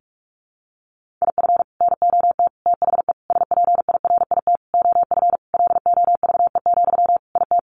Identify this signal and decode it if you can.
Morse code, heard as a keyed tone, about 44 wpm, 700 Hz. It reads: IF NOT THE SPIRIT OF LOVE, IT
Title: Sentences from Top 1400 Words 45wpm